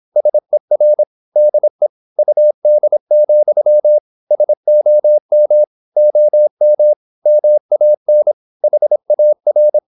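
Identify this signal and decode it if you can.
Morse code, heard as a keyed tone, about 26 wpm, 600 Hz. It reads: SER DE UD, SOM OM MAN HAR
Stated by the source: High-speed (QRQ) Morse at 26 wpm.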